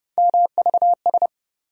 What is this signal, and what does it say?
Morse code, heard as a keyed tone, about 30 wpm, 700 Hz. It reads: MVS
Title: USA Call Sign Postfixes - 3 Letters 30wpm